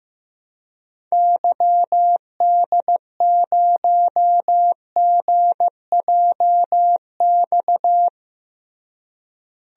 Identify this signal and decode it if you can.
Morse code, heard as a keyed tone, about 15 wpm, 700 Hz. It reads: YD0GJX
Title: DXCC-100 Call Signs - 6 Characters - Repeated 2x - 15wpm